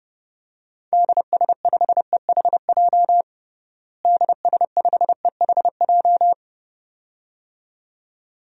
Morse code, heard as a keyed tone, about 30 wpm, 700 Hz. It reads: DS5EHJ DS5EHJ
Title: DXCC-100 Call Signs - 6 Characters - Repeated 2x - 30wpm